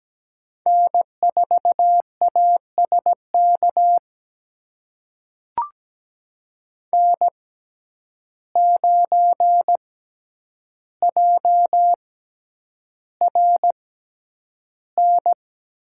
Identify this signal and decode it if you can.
Morse code, heard as a keyed tone, about 17 wpm, spaced as for 9 wpm, 700 Hz. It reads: N4ASK N9JRN
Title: USA Call Signs - 5 Characters - 6x Character Spacing - 17wpm